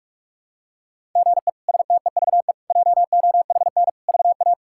Morse code, heard as a keyed tone, about 46 wpm, 700 Hz. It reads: GE STEVE JOHN VA
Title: K1USN SST QSO Practice - 45wpm